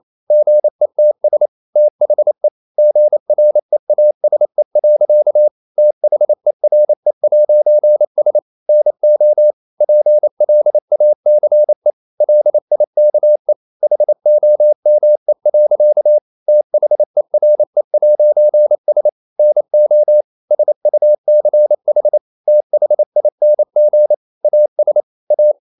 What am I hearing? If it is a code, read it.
Morse code, heard as a keyed tone, about 28 wpm, 600 Hz. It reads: GETS THE GREASE. THERE'S NO PLACE LIKE HOME. THERE'S NO SUCH THING AS A
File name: CW / Practice / EnglishSayings28